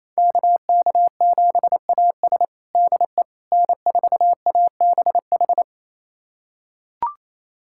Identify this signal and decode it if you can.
Morse code, heard as a keyed tone, about 28 wpm, 700 Hz. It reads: KK7AS DE N4ABH
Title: USA Callsign DE Callsign - 5 Characters - 28wpm